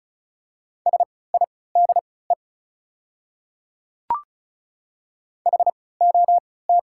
Morse code, heard as a keyed tone, about 34 wpm, spaced as for 20 wpm, 690 Hz. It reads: SIDE HOT
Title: 3x Character Spacing - Top 100 Words - 35wpm